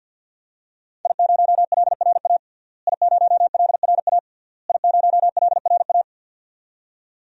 Morse code, heard as a keyed tone, about 52 wpm, 700 Hz. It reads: I0LRA I0LRA I0LRA
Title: DXCC-100 Call Signs - 5 Characters - Repeated 3x - 50wpm